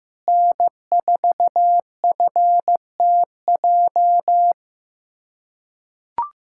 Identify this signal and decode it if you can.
Morse code, heard as a keyed tone, about 16 wpm, 700 Hz. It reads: N4FTJ
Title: USA Callsign DE Callsign - 5 Characters - 15wpm